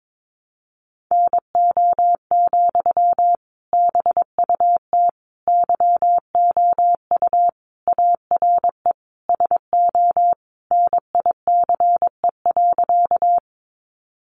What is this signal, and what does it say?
Morse code, heard as a keyed tone, about 22 wpm, 700 Hz. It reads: NO, BUT YOU ARE SO NICE.
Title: Sentences from Top 1100 Words 22wpm